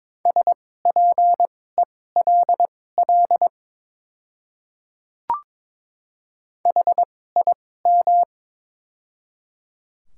Morse code, heard as a keyed tone, about 22 wpm, spaced as for 16 wpm, 700 Hz. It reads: SPELL HIM
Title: 2x Character Spacing - Top 200 Words - 22wpm